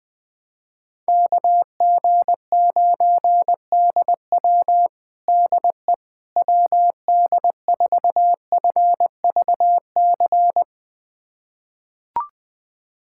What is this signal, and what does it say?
Morse code, heard as a keyed tone, about 20 wpm, 700 Hz. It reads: KG9DW DE WD4FVC
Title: USA Callsign DE Callsign 20wpm